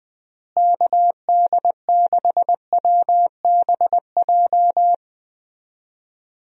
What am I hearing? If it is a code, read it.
Morse code, heard as a keyed tone, about 20 wpm, 700 Hz. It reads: KD6WBJ
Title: USA Call Signs - 6 Characters - Repeated 3x - 20wpm